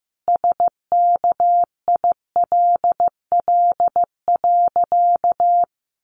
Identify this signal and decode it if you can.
Morse code, heard as a keyed tone, about 15 wpm, 700 Hz. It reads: SKILL.
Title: Sentences from Top 1200 Words 15wpm